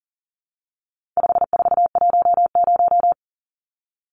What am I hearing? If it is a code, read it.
Morse code, heard as a keyed tone, about 40 wpm, 700 Hz. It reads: 5410
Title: Numbers - 4-Digit 40wpm